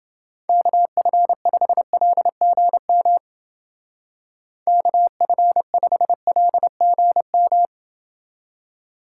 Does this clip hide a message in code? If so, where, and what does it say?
Yes: Morse code from 0.5 to 7.7 s, heard as a keyed tone, about 28 wpm, 700 Hz. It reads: KF5LGM KF5LGM